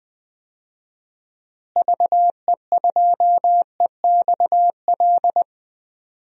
Morse code, heard as a keyed tone, about 20 wpm, 700 Hz. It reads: VE2EXL
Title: DXCC-100 Call Signs - 6 Characters - Repeated 3x - 20wpm